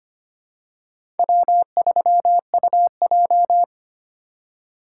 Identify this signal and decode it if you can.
Morse code, heard as a keyed tone, about 25 wpm, 700 Hz. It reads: W3UJ
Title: USA Call Signs - 4 Characters - 8x Character Spacing - 25wpm